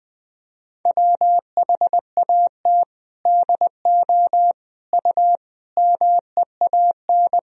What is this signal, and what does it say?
Morse code, heard as a keyed tone, about 19 wpm, 700 Hz. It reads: WHAT DO U MEAN
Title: Sentences from Top 200 Words and Abbreviations 20wpm